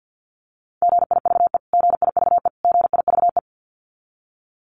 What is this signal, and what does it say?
Morse code, heard as a keyed tone, about 50 wpm, 700 Hz. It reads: GIVE GIVE GIVE
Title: Top 500 Words Repeated 50wpm